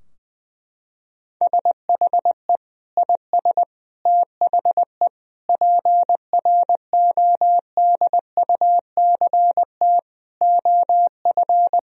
Morse code, heard as a keyed tone, about 20 wpm, 700 Hz. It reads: SHE IS THE PRODUCT OF
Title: Sentences from Top 1000 Words - Review Part 2 - 20wpm